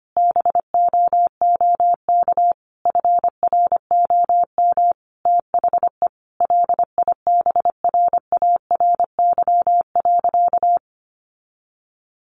Morse code, heard as a keyed tone, about 25 wpm, 700 Hz. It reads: BOOK FROM THE LIBRARY.
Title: Sentences from Top 1300 Words 25wpm